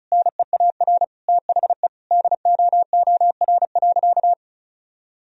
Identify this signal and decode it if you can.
Morse code, heard as a keyed tone, about 36 wpm, 700 Hz. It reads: NEAR THE DOOR.
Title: Sentences from Top 1000 Words - Review Part 2 - 35wpm